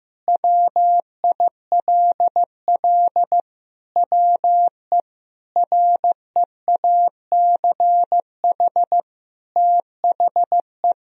Morse code, heard as a keyed tone, about 16 wpm, 700 Hz. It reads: WILL WE REACH THE
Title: Sentences from Top 1000 Words - Review Part 4 - 15wpm